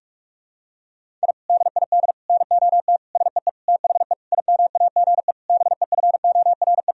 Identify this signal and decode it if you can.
Morse code, heard as a keyed tone, about 46 wpm, 700 Hz. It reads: I DID NOT SEE THE IMAGE BEFORE
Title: Sentences from Top 1000 Words - Review Part 4 - 45wpm